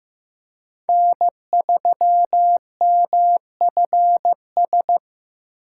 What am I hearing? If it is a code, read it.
Morse code, heard as a keyed tone, about 18 wpm, 700 Hz. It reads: N3MFS
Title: Speed Racing - USA Call Signs 15wpm